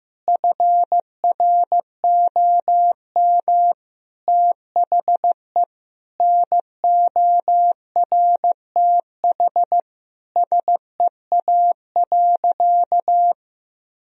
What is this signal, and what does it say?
Morse code, heard as a keyed tone, about 16 wpm, 700 Hz. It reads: FROM THE NORTH SEA.
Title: Sentences from Top 1000 Words - Review Part 1 - 15wpm